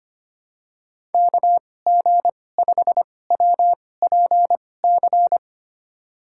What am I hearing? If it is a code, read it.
Morse code, heard as a keyed tone, about 25 wpm, spaced as for 18 wpm, 700 Hz. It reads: KG5WPC
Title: USA Call Signs - 2x Character Spacing - 25wpm